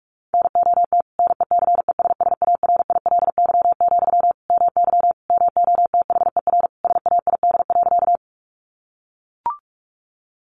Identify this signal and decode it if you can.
Morse code, heard as a keyed tone, about 44 wpm, 700 Hz. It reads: NOT NECESSARILY, MY MOTHER SAID.
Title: Sentences from Top 1400 Words 45wpm